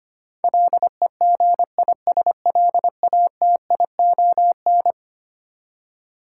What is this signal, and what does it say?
Morse code, heard as a keyed tone, about 25 wpm, 700 Hz. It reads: LEGISLATION
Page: morse-code-ninja-practice-25wpm